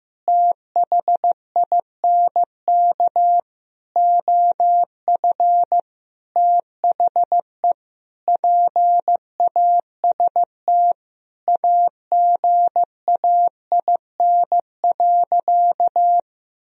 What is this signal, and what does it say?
Morse code, heard as a keyed tone, about 16 wpm, 700 Hz. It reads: THINK OF THE PAST AGAIN.
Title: Sentences from Top 1000 Words - Review Part 3 - 15wpm